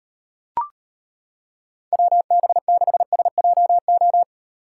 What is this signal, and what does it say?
Morse code, heard as a keyed tone, about 31 wpm, 700 Hz. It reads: WB6SJO
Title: Speed Racing - USA Call Signs 25wpm